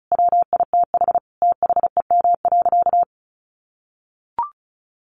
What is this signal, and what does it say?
Morse code, heard as a keyed tone, about 34 wpm, 700 Hz. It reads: WITH THEM.
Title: Sentences from Top 800 Words 35wpm